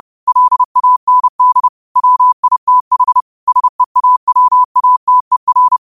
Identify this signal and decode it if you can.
Morse code, heard as a keyed tone, about 30 wpm, 1000 Hz. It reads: LAND WITH SEAWATER
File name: qotdfiles / 30wpm